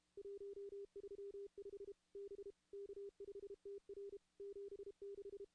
Morse code, heard as a keyed tone, about 32 wpm, 400 Hz. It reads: 135 B K5TR 76
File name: contests / CW-SS-k5tr- -2200